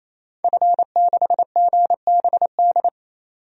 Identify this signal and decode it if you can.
Morse code, heard as a keyed tone, about 28 wpm, 700 Hz. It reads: F6GBD
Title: DXCC-100 Call Signs - 5 Characters - 28wpm